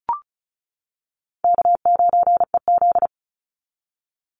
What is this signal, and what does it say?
Morse code, heard as a keyed tone, about 34 wpm, 700 Hz. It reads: K9EZ
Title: NA SOTA Chaser Callsigns 35wpm